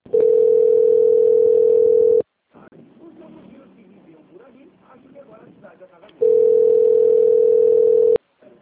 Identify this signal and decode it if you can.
Ringback tone (North America), 440 + 480 Hz, with speech.
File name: out- - - - - .43248